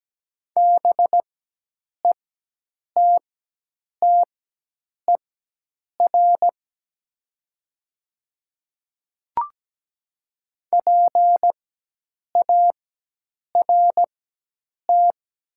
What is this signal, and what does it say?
Morse code, heard as a keyed tone, about 17 wpm, spaced as for 8 wpm, 700 Hz. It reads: BETTER PART
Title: 4x Character Spacing - Top 500 Words - 17wpm